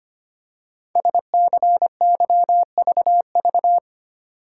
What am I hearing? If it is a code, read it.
Morse code, heard as a keyed tone, about 25 wpm, 700 Hz. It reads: SCYVV
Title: Code Groups - 5 Letters - 25wpm